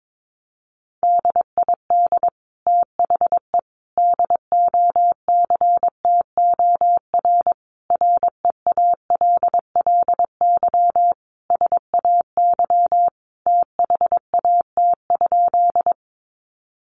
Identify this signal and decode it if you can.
Morse code, heard as a keyed tone, about 22 wpm, 700 Hz. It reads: DID THE DOCTOR REALLY SAY THAT?
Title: Sentences from Top 1500 Words 22wpm